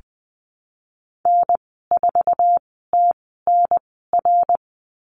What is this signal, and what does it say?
Morse code, heard as a keyed tone, about 19 wpm, spaced as for 14 wpm, 700 Hz. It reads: N4TNR
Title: 088 Lesson 15g - Number 5 - Call Signs 20wpm